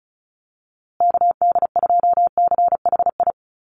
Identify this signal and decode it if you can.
Morse code, heard as a keyed tone, about 34 wpm, 700 Hz. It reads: KD2CHI
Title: USA Call Signs Repeated 2x 35wpm